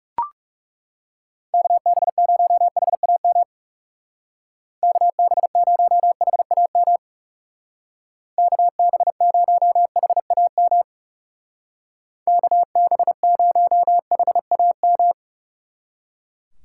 Morse code, heard as a keyed tone, about 35 wpm, 700 Hz. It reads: KB0HAM KB0HAM KB0HAM KB0HAM
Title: Speed Racing - USA 2x3 Call Signs 30wpm